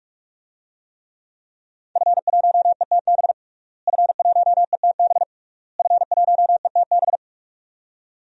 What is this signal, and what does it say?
Morse code, heard as a keyed tone, about 43 wpm, 700 Hz. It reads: F1ETB F1ETB F1ETB
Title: DXCC-100 Call Signs - 5 Characters - Repeated 3x - 45wpm